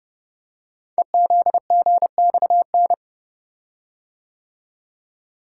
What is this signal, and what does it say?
Morse code, heard as a keyed tone, about 30 wpm, 700 Hz. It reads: EZGXN
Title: Code Groups - 5 Letter:Numbers - 30wpm